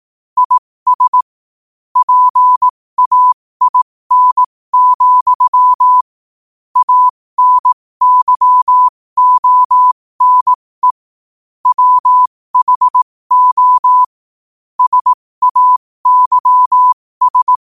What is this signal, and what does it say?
Morse code, heard as a keyed tone, about 18 wpm, spaced as for 15 wpm, 1000 Hz. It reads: IS PAIN, ANYONE WHO SAYS
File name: qotdfiles / 15wpm